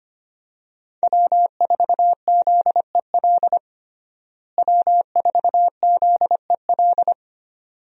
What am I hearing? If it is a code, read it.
Morse code, heard as a keyed tone, about 25 wpm, 700 Hz. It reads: W4ZEL W4ZEL
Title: USA Call Signs - 5 Characters - Repeated 2x - 25wpm